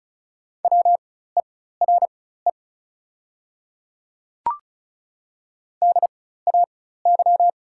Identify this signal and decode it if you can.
Morse code, heard as a keyed tone, about 36 wpm, spaced as for 16 wpm, 700 Hz. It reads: WERE DAY
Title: 4x Character Spacing - Top 100 Words - 35wpm